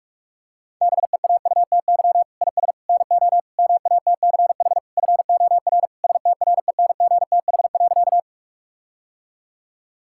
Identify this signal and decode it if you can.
Morse code, heard as a keyed tone, about 46 wpm, 700 Hz. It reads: BEAUTY IS NO MATCH FOR STRENGTH.
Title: Sentences from Top 1600 Words 45wpm